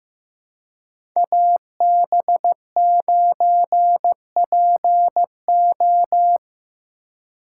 Morse code, heard as a keyed tone, about 17 wpm, 700 Hz. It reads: AB9PO
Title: Speed Racing - USA 2x2 Call Signs 15wpm